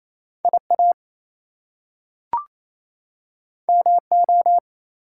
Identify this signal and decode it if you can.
Morse code, heard as a keyed tone, about 28 wpm, 700 Hz. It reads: IA MO
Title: US States and Canadian Provinces 28wpm